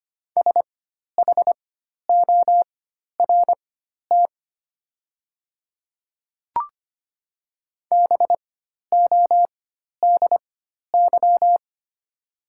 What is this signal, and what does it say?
Morse code, heard as a keyed tone, about 25 wpm, spaced as for 12 wpm, 700 Hz. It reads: SHORT BODY
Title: 4x Character Spacing - Top 500 Words - 25wpm